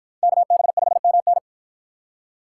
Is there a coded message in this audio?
Yes: Morse code from 0.2 to 1.4 s, heard as a keyed tone, about 48 wpm, 700 Hz. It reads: KB5MN